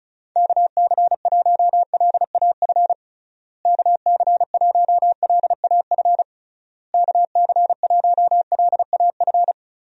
Morse code, heard as a keyed tone, about 36 wpm, 700 Hz. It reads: KC1LAF KC1LAF KC1LAF
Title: USA Call Signs - 6 Characters - Repeated 3x - 35wpm